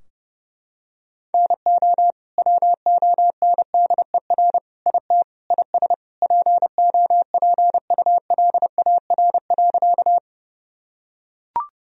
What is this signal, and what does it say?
Morse code, heard as a keyed tone, about 30 wpm, 700 Hz. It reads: NO WONDER IT IS POPULAR.
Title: Sentences from Top 1000 Words - Review Part 3 - 30wpm